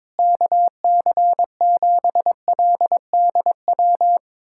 Morse code, heard as a keyed tone, about 22 wpm, 700 Hz. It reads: KC7LDW
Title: USA Callsign DE Callsign - 6 Characters - 22wpm